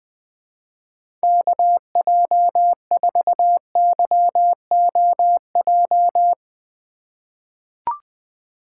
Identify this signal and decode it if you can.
Morse code, heard as a keyed tone, about 20 wpm, 700 Hz. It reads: KJ4YOJ
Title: USA Call Signs - 6 Characters - 3x Character Spacing - 20wpm